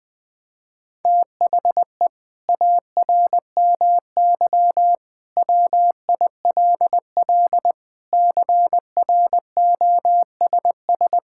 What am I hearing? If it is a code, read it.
Morse code, heard as a keyed tone, about 20 wpm, 700 Hz. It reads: THE ARMY WILL CROSS
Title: Sentences from Top 1000 Words - Review Part 5 - 20wpm